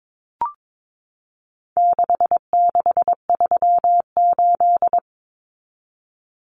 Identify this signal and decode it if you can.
Morse code, heard as a keyed tone, about 22 wpm, 700 Hz. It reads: 6638
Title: Numbers - 4-Digit 22wpm